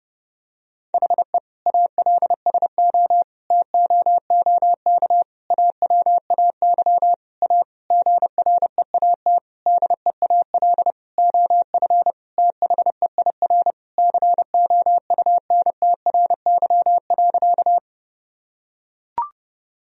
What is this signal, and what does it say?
Morse code, heard as a keyed tone, about 30 wpm, 700 Hz. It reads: HE ALSO TOOK AWAY A GREAT DEAL OF THEIR COUNTRY.
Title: Sentences from Top 1000 Words - Review Part 1 - 30wpm